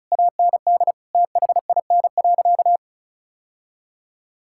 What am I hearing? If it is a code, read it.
Morse code, heard as a keyed tone, about 36 wpm, 700 Hz. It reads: AND THIN.
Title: Sentences from Top 1600 Words 35wpm